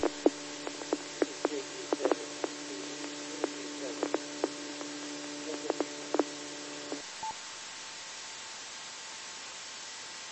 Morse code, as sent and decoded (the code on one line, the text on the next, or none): none